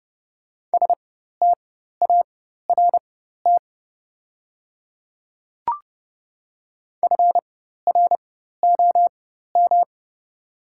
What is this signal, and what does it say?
Morse code, heard as a keyed tone, about 30 wpm, spaced as for 14 wpm, 690 Hz. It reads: START FROM
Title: 4x Character Spacing - Top 500 Words - 30wpm